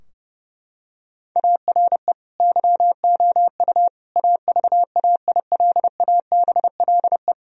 Morse code, heard as a keyed tone, about 30 wpm, 700 Hz. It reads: ARE YOU AVAILABLE
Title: Sentences from Top 1000 Words - Review Part 3 - 30wpm